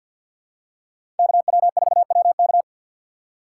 Morse code, heard as a keyed tone, about 53 wpm, 700 Hz. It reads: KW4WK
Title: USA Call Signs - 4x Character Spacing - 50wpm